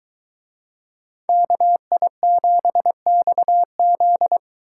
Morse code, heard as a keyed tone, about 24 wpm, 700 Hz. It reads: KI7XZ